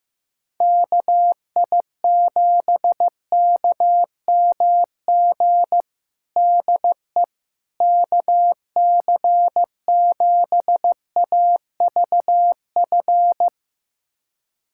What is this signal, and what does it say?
Morse code, heard as a keyed tone, about 15 wpm, 700 Hz. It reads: KI7KMG DE KC7AVF
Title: USA Callsign DE Callsign - 6 Characters - 15wpm